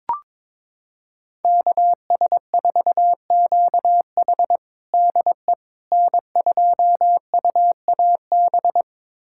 Morse code, heard as a keyed tone, about 21 wpm, 700 Hz. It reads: KS4QH DE N2UAB
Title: USA Callsign DE Callsign - 5 Characters - 22wpm